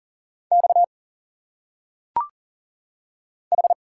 Morse code, heard as a keyed tone, about 40 wpm, 700 Hz. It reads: X H
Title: Single Letters - Phonetic Pronunciation 40wpm